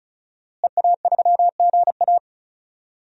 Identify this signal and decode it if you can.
Morse code, heard as a keyed tone, about 36 wpm, 700 Hz. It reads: EA3GA
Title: DXCC-100 Call Signs - 5 Characters - Repeated 3x - 35wpm